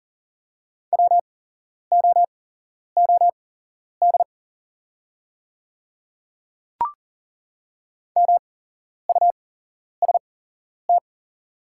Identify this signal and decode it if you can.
Morse code, heard as a keyed tone, about 40 wpm, spaced as for 11 wpm, 700 Hz. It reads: WOOD MUST
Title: 8x Character Spacing - Top 300 Words - 40wpm